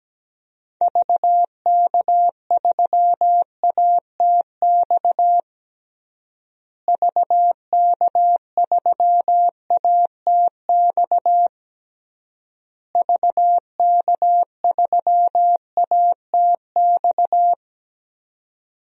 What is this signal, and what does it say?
Morse code, heard as a keyed tone, about 17 wpm, 700 Hz. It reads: VK3ATX VK3ATX VK3ATX
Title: DXCC-100 Call Signs - 6 Characters - Repeated 3x - 17wpm